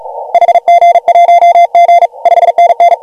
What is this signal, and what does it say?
Morse code, heard as a keyed tone, about 36 wpm, 690 Hz. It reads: HG1G 5NN